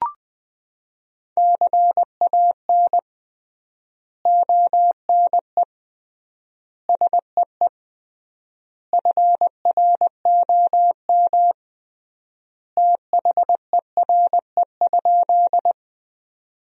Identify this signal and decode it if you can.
Morse code, heard as a keyed tone, about 20 wpm, 700 Hz. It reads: CAN ONE SEE FROM THERE?
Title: 3x Word-Spacing - Sentences from Top 100 Words - 20wpm